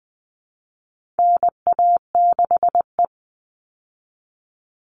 Morse code, heard as a keyed tone, about 20 wpm, 700 Hz. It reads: NA6E
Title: All Cut Numbers - 4-Digit 20wpm